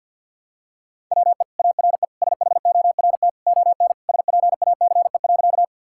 Morse code, heard as a keyed tone, about 50 wpm, 700 Hz. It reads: WE ARE SHORT ON SPACE.
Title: Sentences from Top 1000 Words - Review Part 5 - 50wpm